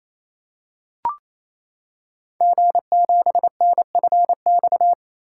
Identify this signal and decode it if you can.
Morse code, heard as a keyed tone, about 28 wpm, 700 Hz. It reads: G7NFX
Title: Code Groups - 5 Letter:Numbers - 28wpm